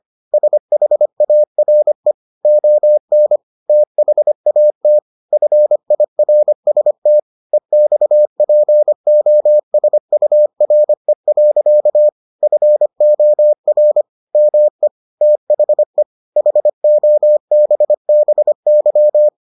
Morse code, heard as a keyed tone, about 25 wpm, 600 Hz. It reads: SHARE ON THAT FIRST EXPOSURE. FOR ME THE HOBBY